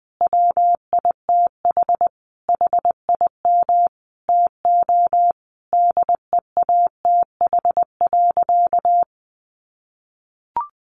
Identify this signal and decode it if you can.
Morse code, heard as a keyed tone, about 20 wpm, 700 Hz. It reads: WITH HIM TO DEATH.
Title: Sentences from Top 1100 Words 20wpm